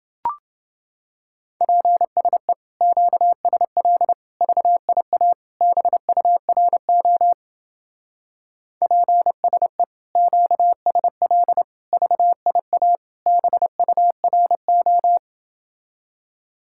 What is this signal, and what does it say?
Morse code, heard as a keyed tone, about 30 wpm, 700 Hz. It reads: PSE QSL VIA BURO PSE QSL VIA BURO